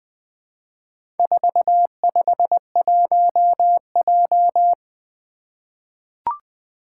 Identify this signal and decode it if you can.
Morse code, heard as a keyed tone, about 21 wpm, 700 Hz. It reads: 451J
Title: Four Challenging Letter-Number Combinations 20wpm